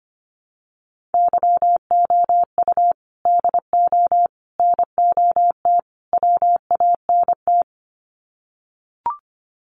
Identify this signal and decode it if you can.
Morse code, heard as a keyed tone, about 25 wpm, 700 Hz. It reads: YOU DO NOT WANT
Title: Sets of 4 Words 25wpm